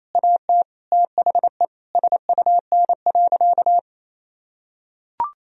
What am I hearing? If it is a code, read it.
Morse code, heard as a keyed tone, about 28 wpm, 700 Hz. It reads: AT THE SUN.
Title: Sentences from Top 1000 Words - Review Part 1 - 28wpm